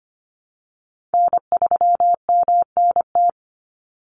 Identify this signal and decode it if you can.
Morse code, heard as a keyed tone, about 25 wpm, 700 Hz. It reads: N3MNT
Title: Call Signs Repeated 3x - 25wpm